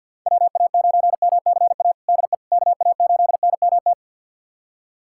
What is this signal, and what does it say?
Morse code, heard as a keyed tone, about 53 wpm, 700 Hz. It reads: WA9MCA DE KA8NMT
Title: USA Callsign DE Callsign - 6 Characters - 50wpm